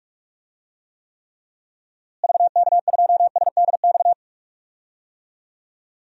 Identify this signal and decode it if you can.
Morse code, heard as a keyed tone, about 46 wpm, 700 Hz. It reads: VK2SDX
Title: DXCC-100 Call Signs - 6 Characters - Repeated 3x - 45wpm